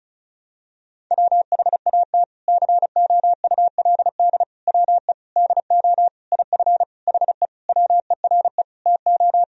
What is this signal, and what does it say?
Morse code, heard as a keyed tone, about 36 wpm, 700 Hz. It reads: WHAT COULD WE DO IF HE WERE TO
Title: Sentences from Top 400 Words and Abbreviations 35wpm